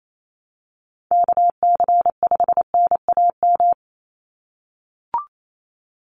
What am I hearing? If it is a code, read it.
Morse code, heard as a keyed tone, about 28 wpm, 700 Hz. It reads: KC5NAM
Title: USA Call Signs 28wpm